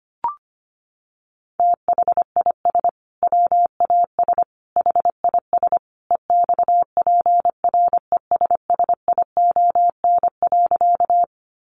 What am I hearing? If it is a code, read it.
Morse code, heard as a keyed tone, about 25 wpm, 700 Hz. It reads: THIS WAS HIS EXPRESSION.
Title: Sentences from Top 1200 Words 25wpm